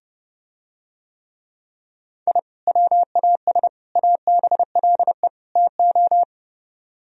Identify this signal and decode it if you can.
Morse code, heard as a keyed tone, about 30 wpm, 700 Hz. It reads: I WAS ABLE TO